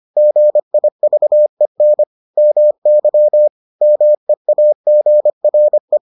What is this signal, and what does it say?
Morse code, heard as a keyed tone, about 25 wpm, 600 Hz. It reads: GIVEN MY MEAGRE